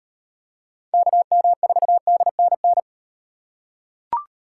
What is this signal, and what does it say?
Morse code, heard as a keyed tone, about 38 wpm, 700 Hz. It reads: KM4DNN